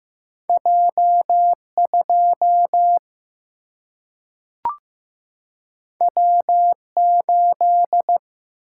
Morse code, heard as a keyed tone, about 15 wpm, 700 Hz. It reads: J 2 W 8
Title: Single-Letter Single-Number 15wpm